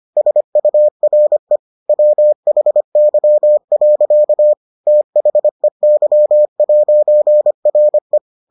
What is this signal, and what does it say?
Morse code, heard as a keyed tone, about 25 wpm, 600 Hz. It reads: SURE WHY. THEY'RE